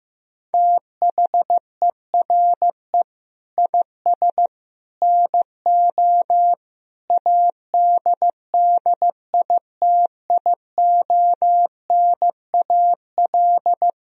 Morse code, heard as a keyed tone, about 16 wpm, 700 Hz. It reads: THERE IS NO ADDITIONAL
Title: Sentences from Top 1000 Words - Review Part 4 - 15wpm